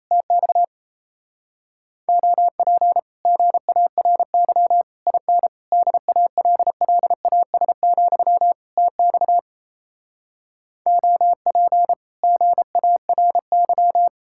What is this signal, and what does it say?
Morse code, heard as a keyed tone, about 37 wpm, 700 Hz. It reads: TX OP GARY IN DALLAS, TX OP GARY